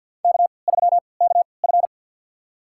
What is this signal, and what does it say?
Morse code, heard as a keyed tone, about 52 wpm, spaced as for 29 wpm, 700 Hz. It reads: K3KF
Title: USA Call Signs - 4 Characters - 3x Character Spacing - 50wpm